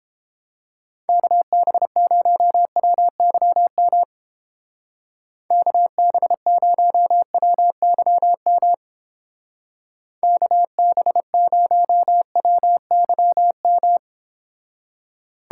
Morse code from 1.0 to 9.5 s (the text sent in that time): KB0WYM KB0WYM